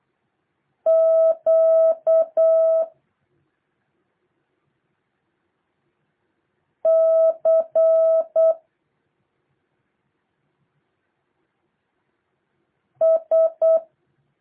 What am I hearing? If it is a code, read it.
Morse code, heard as a keyed tone, about 8 wpm, 650 Hz. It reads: Q C S
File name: jh8cbh / 2023cw